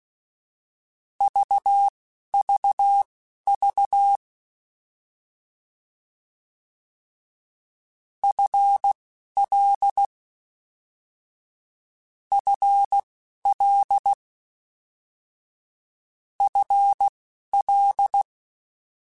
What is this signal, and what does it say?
Morse code, heard as a keyed tone, about 16 wpm, spaced as for 12 wpm, 800 Hz. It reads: VVV FL FL FL
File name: Combinations / Combinations-mp3s / Combinations@10